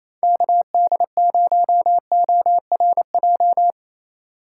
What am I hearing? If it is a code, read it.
Morse code, heard as a keyed tone, about 28 wpm, 700 Hz. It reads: KD0ORJ
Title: USA Call Signs - 6 Characters - 3x Character Spacing - 28wpm